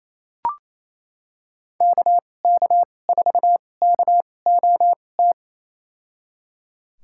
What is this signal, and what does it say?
Morse code, heard as a keyed tone, about 28 wpm, spaced as for 20 wpm, 700 Hz. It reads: KK4KOT
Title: USA Call Signs - 2x Character Spacing - 28wpm